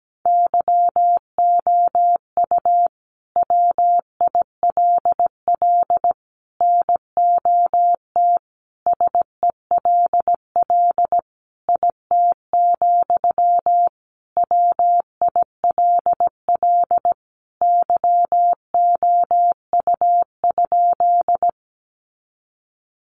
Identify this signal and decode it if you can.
Morse code, heard as a keyed tone, about 17 wpm, 700 Hz. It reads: YOU WILL NOT SELL IT, WILL YOU?
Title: Sentences from Top 1100 Words 17wpm